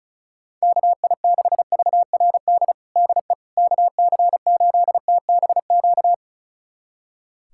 Morse code, heard as a keyed tone, about 36 wpm, 700 Hz. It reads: KI6VRD DE KC8TBQ
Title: USA Callsign DE Callsign 35wpm